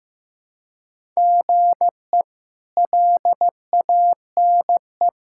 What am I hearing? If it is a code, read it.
Morse code, heard as a keyed tone, about 15 wpm, 700 Hz. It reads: GE LANE
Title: K1USN SST QSO Practice - 15wpm